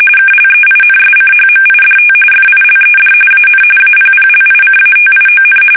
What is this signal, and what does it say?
RTTY, about 45.45 baud. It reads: KLEINSCHMIDT FILED AN APPLICATION
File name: fsk850 / teletype-corp